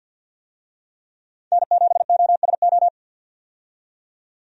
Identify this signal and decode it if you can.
Morse code, heard as a keyed tone, about 52 wpm, 700 Hz. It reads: N7OSO
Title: USA Call Signs - 5 Characters - 50wpm